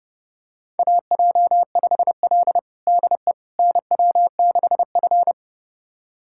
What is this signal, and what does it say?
Morse code, heard as a keyed tone, about 30 wpm, 700 Hz. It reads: AJ5L DE NW6F
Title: USA Callsign DE Callsign - 4 Characters - 30wpm